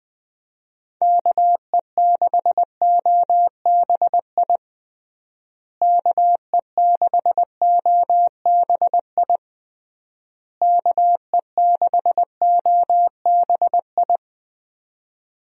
Morse code, heard as a keyed tone, about 20 wpm, 700 Hz. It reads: KE6OBI KE6OBI KE6OBI
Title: USA Call Signs - 6 Characters - Repeated 3x - 20wpm